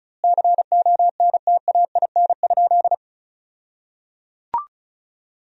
Morse code, heard as a keyed tone, about 34 wpm, 700 Hz. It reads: CONTAIN?
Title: 2x Word-Spacing - Sentences from Top 500 Words - 35wpm